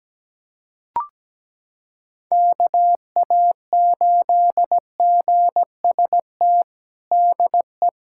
Morse code, heard as a keyed tone, about 17 wpm, 700 Hz. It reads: KA8GST DE
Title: USA Callsign DE Callsign - 6 Characters - 17wpm